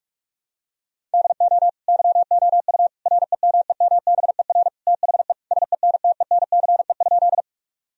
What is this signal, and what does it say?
Morse code, heard as a keyed tone, about 46 wpm, 700 Hz. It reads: DO YOU REMEMBER THE SENTENCE?
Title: Sentences from Top 1000 Words - Review Part 1 - 45wpm